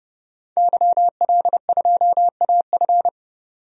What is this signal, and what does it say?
Morse code, heard as a keyed tone, about 30 wpm, 700 Hz. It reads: YL2AF
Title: DXCC-100 Call Signs - 5 Characters - 30wpm